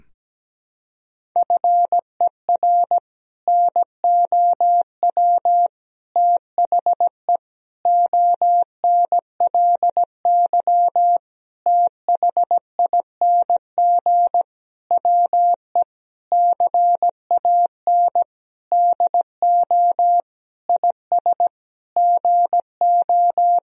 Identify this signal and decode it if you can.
Morse code, heard as a keyed tone, about 17 wpm, 700 Hz. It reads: FER NOW THE ONLY THING WE CAN DO IS GO
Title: Sentences from Top 200 Words and Abbreviations 17wpm